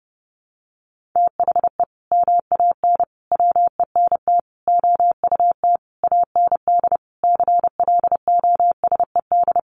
Morse code, heard as a keyed tone, about 30 wpm, 700 Hz. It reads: THE MAN WENT OUT AND CLOSED
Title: Sentences from Top 1400 Words 30wpm